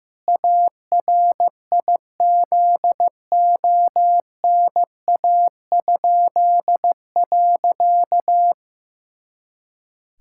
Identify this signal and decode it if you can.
Morse code, heard as a keyed tone, about 15 wpm, 700 Hz. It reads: ARIZONA?.
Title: Pangram Challenge v2 - 15wpm